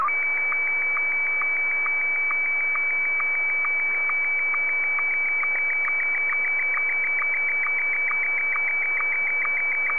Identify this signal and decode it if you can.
SSTV (slow-scan television), Martin 1.